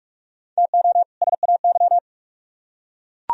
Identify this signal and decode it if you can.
Morse code, heard as a keyed tone, about 44 wpm, 700 Hz. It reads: TO SAY